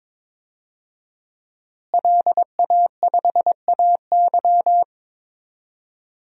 Morse code, heard as a keyed tone, about 22 wpm, 700 Hz. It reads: LA5AY
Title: DXCC-100 Call Signs - 5 Characters - Repeated 2x - 22wpm